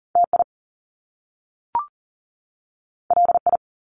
Morse code, heard as a keyed tone, about 42 wpm, 700 Hz. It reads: TI LI
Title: Top Two Letter Combinations in Words 40wpm